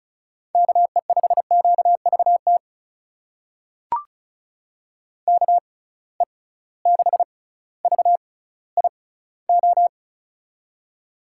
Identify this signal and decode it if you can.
Morse code, heard as a keyed tone, about 36 wpm, spaced as for 18 wpm, 700 Hz. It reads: KE5QVT KE6VIO
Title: USA Call Signs - 6 Characters - 6x Character Spacing - 35wpm